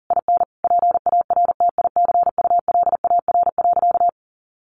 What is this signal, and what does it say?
Morse code, heard as a keyed tone, about 40 wpm, 700 Hz. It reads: IN PARTICULAR.
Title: Sentences from Top 900 Words 40wpm